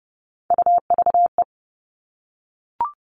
Morse code, heard as a keyed tone, about 30 wpm, 700 Hz. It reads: UVE
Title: All Cut Numbers - 3-Digit 30wpm